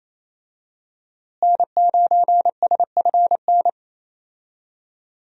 Morse code, heard as a keyed tone, about 28 wpm, 700 Hz. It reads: N9SFN
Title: USA Call Signs - 5 Characters - Repeated 3x - 28wpm